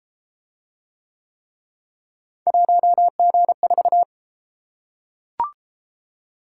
Morse code, heard as a keyed tone, about 33 wpm, 700 Hz. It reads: 1G4